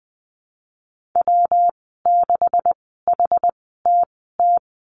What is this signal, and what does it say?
Morse code, heard as a keyed tone, about 20 wpm, spaced as for 14 wpm, 700 Hz. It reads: W6HTT
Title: 140 Lesson 23g - Letter f - Call Signs 20wpm